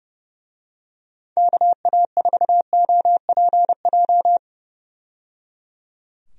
Morse code, heard as a keyed tone, about 30 wpm, 700 Hz. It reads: KA4OPJ
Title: USA Call Signs - 6 Characters - 30wpm